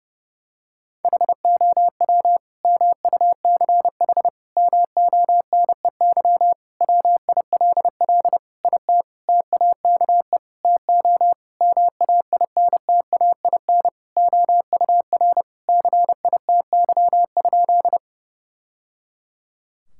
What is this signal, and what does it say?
Morse code, heard as a keyed tone, about 30 wpm, 700 Hz. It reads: HOW MUCH MONEY WILL IT TAKE TO MAINTAIN OUR CITY?
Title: Sentences from Top 1000 Words - Review Part 5 - 30wpm